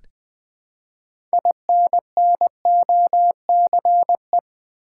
Morse code, heard as a keyed tone, about 20 wpm, 700 Hz. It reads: INNOCE
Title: Build A Word - 3 Letters At A Time - 8-Letter Words - 20wpm